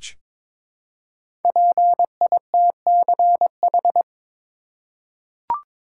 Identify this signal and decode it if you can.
Morse code, heard as a keyed tone, about 22 wpm, 690 Hz. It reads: PITCH
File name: 5 Letter Words - First 3 Letters Given - 22wpm